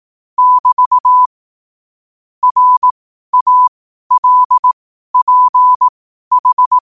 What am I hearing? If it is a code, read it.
Morse code, heard as a keyed tone, about 18 wpm, spaced as for 12 wpm, 1000 Hz. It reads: <BT> RALPH
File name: qotdfiles / 12wpm